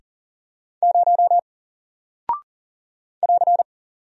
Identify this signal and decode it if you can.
Morse code, heard as a keyed tone, about 40 wpm, 700 Hz. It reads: 0 <AR>
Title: Single Character (Letter:Number:Punctuation:Prosign) Alt 40wpm